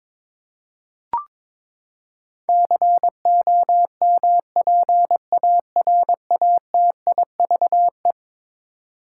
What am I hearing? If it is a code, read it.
Morse code, heard as a keyed tone, about 22 wpm, 700 Hz. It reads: COMPARATIVE
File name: practice / 22wpm / Eleven Letter Words - 22wpm